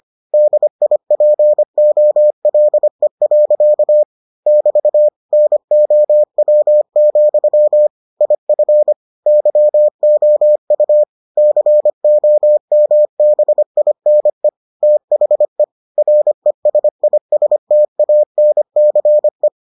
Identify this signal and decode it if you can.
Morse code, heard as a keyed tone, about 25 wpm, 600 Hz. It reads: DIPOLE. <BT> NOW, IF YOU COMBINE THE RESISTANCE